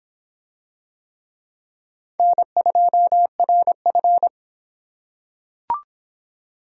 Morse code, heard as a keyed tone, about 23 wpm, 700 Hz. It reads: N2RF